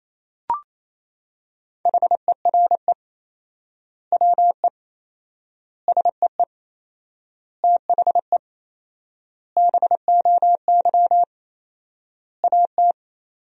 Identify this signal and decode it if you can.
Morse code, heard as a keyed tone, about 28 wpm, 700 Hz. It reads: HERE WE SEE THE BOY AT
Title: 4x Word-Spacing - Sentences from Top 300 Words - 28wpm